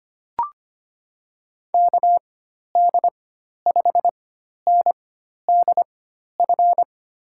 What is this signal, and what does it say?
Morse code, heard as a keyed tone, about 25 wpm, spaced as for 12 wpm, 700 Hz. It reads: KD5NDF
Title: USA Call Signs - 6 Characters - 4x Character Spacing - 25wpm